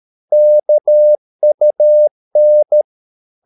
Morse code, heard as a keyed tone, about 13 wpm, 600 Hz. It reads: KUN
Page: Ferie 13 wpm